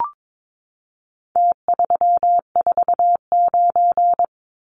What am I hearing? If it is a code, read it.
Morse code, heard as a keyed tone, about 22 wpm, 700 Hz. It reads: T349
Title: Numbers - Contest Style - 4-Digit 22wpm